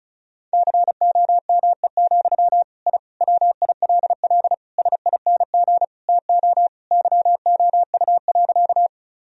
Morse code, heard as a keyed tone, about 34 wpm, 700 Hz. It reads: COME, I WILL SING TO YOU.
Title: Sentences from Top 1000 Words - Review Part 2 - 35wpm